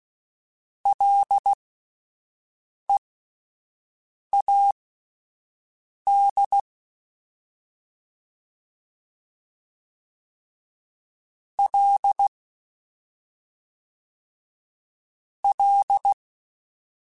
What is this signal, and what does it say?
Morse code, heard as a keyed tone, about 16 wpm, spaced as for 5 wpm, 800 Hz. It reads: LEAD L L
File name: Ls / Single L m3s / Single L Prefixe 3x2 Twice @5